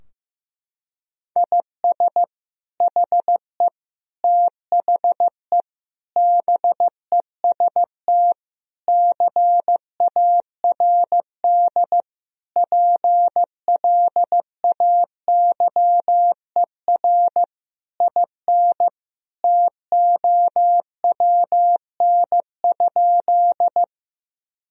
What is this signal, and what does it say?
Morse code, heard as a keyed tone, about 15 wpm, 700 Hz. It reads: IS HE THE BEST CARD PLAYER IN TOWN?
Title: Sentences from Top 1000 Words - Review Part 4 - 15wpm